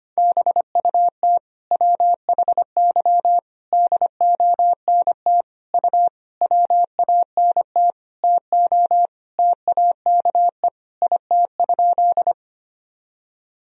Morse code, heard as a keyed tone, about 25 wpm, 700 Hz. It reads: BUT WHY DONT U WANT TO TAKE IT?
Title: Sentences from Top 300 Words and Abbreviations 25wpm